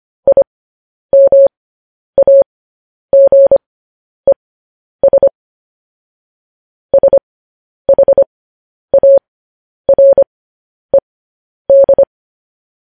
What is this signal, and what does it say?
Morse code, heard as a keyed tone, about 25 wpm, spaced as for 10 wpm, 550 Hz. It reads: IMAGES SHARED
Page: News Headlines in Morse Code at 10 WPM